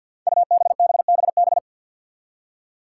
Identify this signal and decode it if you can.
Morse code, heard as a keyed tone, about 45 wpm, 700 Hz. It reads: UBBBB
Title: Five Challenging Letter-Number Combinations 50wpm